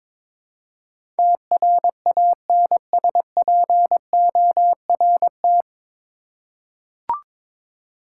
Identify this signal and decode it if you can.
Morse code, heard as a keyed tone, about 22 wpm, 700 Hz. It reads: TRANSPORT
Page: Morse Code Ninja practice 22wpm